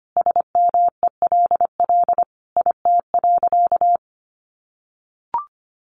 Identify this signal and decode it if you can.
Morse code, heard as a keyed tone, about 25 wpm, 700 Hz. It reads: SMELL IT.
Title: Easy Phrases 25wpm